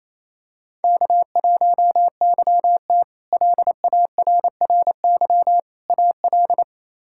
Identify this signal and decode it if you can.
Morse code, heard as a keyed tone, about 27 wpm, 700 Hz. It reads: K1YT LARRY AL
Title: K1USN SST QSO Practice - 28wpm